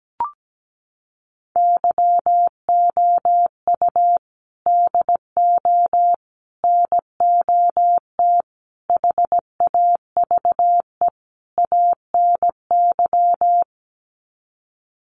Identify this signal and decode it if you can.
Morse code, heard as a keyed tone, about 17 wpm, 700 Hz. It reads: YOU DO NOT HAVE ANY
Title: Sets of 5 Words Encore 17wpm